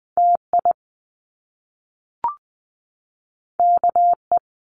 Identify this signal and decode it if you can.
Morse code, heard as a keyed tone, about 20 wpm, 700 Hz. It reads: TI KE
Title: Top Two Letter Combinations in Words 20wpm